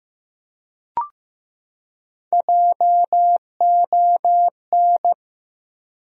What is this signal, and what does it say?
Morse code, heard as a keyed tone, about 15 wpm, 700 Hz. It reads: JON
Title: USA Common Names - Max 4 Letters - 15wpm